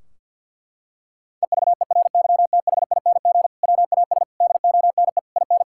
Morse code, heard as a keyed tone, about 50 wpm, 700 Hz. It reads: EVERYTHING WAS DONE IN
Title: Sentences from Top 1600 Words 50wpm